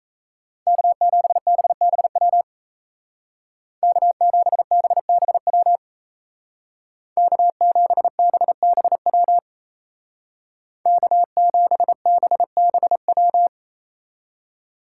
Morse code, heard as a keyed tone, about 34 wpm, 700 Hz. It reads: K7BBW K7BBW K7BBW K7BBW
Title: Speed Racing - USA 1x3 Call Signs 28wpm